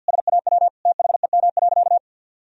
Morse code, heard as a keyed tone, about 49 wpm, 700 Hz. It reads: SAW THEM.
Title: Sentences from Top 1000 Words - Review Part 2 - 50wpm